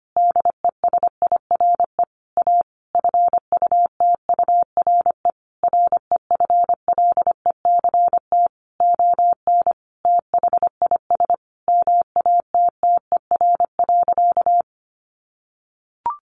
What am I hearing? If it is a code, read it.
Morse code, heard as a keyed tone, about 25 wpm, 700 Hz. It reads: DESIRE A FUTURE REFLECT ON THIS MATTER.
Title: Sentences from Top 1500 Words 25wpm